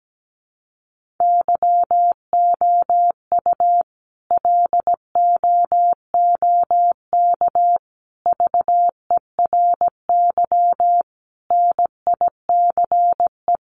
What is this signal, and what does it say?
Morse code, heard as a keyed tone, about 17 wpm, 700 Hz. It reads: YOU LOOK VERY NICE
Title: Sentences from Top 1500 Words 17wpm